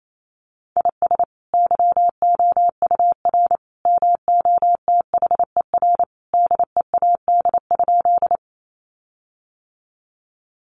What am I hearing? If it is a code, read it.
Morse code, heard as a keyed tone, about 28 wpm, 700 Hz. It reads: IS YOUR MOTHER DEAD?
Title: Sentences from Top 1500 Words 28wpm